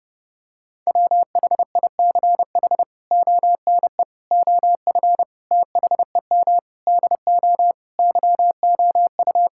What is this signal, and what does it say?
Morse code, heard as a keyed tone, about 30 wpm, 700 Hz. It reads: WHICH ONE OF THEM DO YOU
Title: Sentences from Top 1000 Words - Review Part 5 - 30wpm